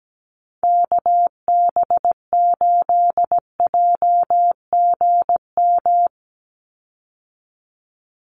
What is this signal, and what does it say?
Morse code, heard as a keyed tone, about 17 wpm, 700 Hz. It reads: KB8JGM
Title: USA Call Signs 17wpm